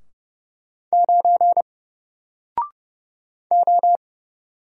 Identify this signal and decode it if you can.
Morse code, heard as a keyed tone, about 29 wpm, 700 Hz. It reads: 9 O
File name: Single-Letter-Number-30wpm